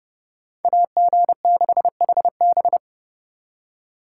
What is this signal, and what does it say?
Morse code, heard as a keyed tone, about 36 wpm, 700 Hz. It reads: AG6HB
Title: Speed Racing - USA 2x2 Call Signs 30wpm